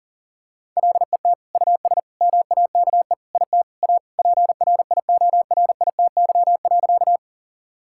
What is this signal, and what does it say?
Morse code, heard as a keyed tone, about 40 wpm, 700 Hz. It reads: LET US MAKE IT A PRIORITY.
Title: Sentences from Top 1600 Words 40wpm